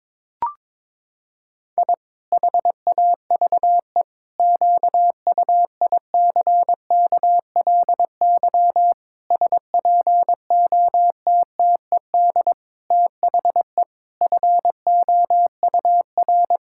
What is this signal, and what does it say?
Morse code, heard as a keyed tone, about 22 wpm, 700 Hz. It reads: I HAVE QUICKLY SPOTTED THE FOUR
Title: Pangram Challenge v2 - 22wpm